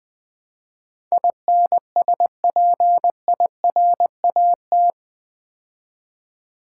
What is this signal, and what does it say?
Morse code, heard as a keyed tone, about 20 wpm, 700 Hz. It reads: INSPIRAT
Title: Build A Word - 4 Letters At A Time - 11-Letter Words - 20wpm